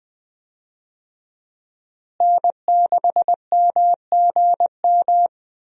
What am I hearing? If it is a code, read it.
Morse code, heard as a keyed tone, about 20 wpm, 700 Hz. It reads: N6MGM
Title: USA Call Signs - 5 Characters - Repeated 3x - 20wpm